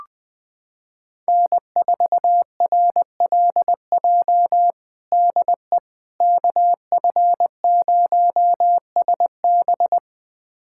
Morse code, heard as a keyed tone, about 20 wpm, 700 Hz. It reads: N4RLJ DE KF0SB
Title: USA Callsign DE Callsign - 5 Characters - 20wpm